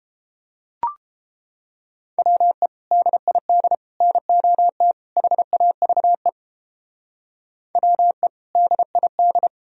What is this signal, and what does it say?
Morse code, heard as a keyed tone, about 31 wpm, 700 Hz. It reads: WE DID NOT HAVE WE DID